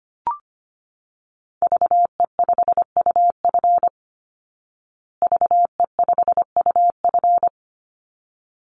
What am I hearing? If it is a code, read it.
Morse code, heard as a keyed tone, about 25 wpm, 700 Hz. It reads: VE5UF VE5UF
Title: Call Signs Repeated 3x - 25wpm